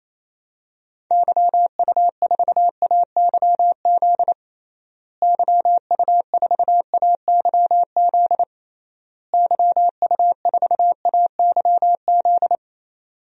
Morse code, heard as a keyed tone, about 28 wpm, 700 Hz. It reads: YU4AYZ YU4AYZ YU4AYZ
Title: DXCC-100 Call Signs - 6 Characters - Repeated 3x - 28wpm